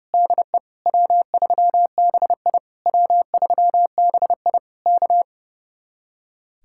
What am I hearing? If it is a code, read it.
Morse code, heard as a keyed tone, about 35 wpm, 700 Hz. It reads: DE W3BI W3BI K
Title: Speed-Racing Common QSO Phrases Encore - 30wpm